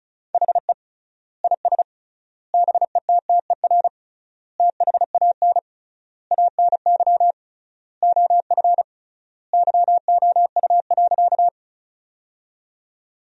Morse code, heard as a keyed tone, about 34 wpm, 700 Hz. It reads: HE IS BETTER THAN ANY OF YOU.
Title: 3x Word-Spacing - Sentences from Top 400 Words - 35wpm